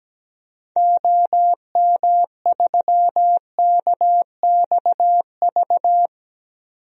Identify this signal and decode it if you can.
Morse code, heard as a keyed tone, about 17 wpm, 700 Hz. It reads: OM3KXV
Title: DXCC-100 Call Signs - 6 Characters - 17wpm